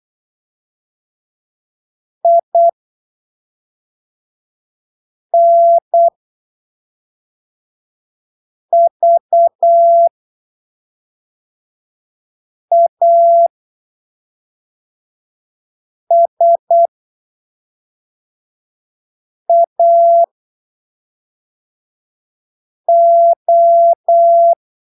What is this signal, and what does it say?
Morse code, heard as a keyed tone, about 8 wpm, spaced as for 3 wpm, 680 Hz. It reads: INVASAO
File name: arquivos / texto / texto 30 08ppm